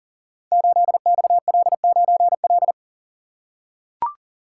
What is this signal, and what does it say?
Morse code, heard as a keyed tone, about 38 wpm, 700 Hz. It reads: 8XL9L
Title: Code Groups - 5 Letter:Numbers - 40wpm